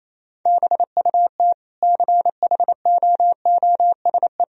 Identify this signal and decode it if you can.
Morse code, heard as a keyed tone, about 28 wpm, 700 Hz. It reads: BUT CHOOSE
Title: Sentences from Top 1000 Words - Review Part 4 - 28wpm